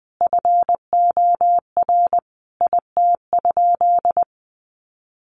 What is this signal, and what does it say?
Morse code, heard as a keyed tone, about 19 wpm, 700 Hz. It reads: FOR IT?
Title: Sentences from Top 400 Words 20wpm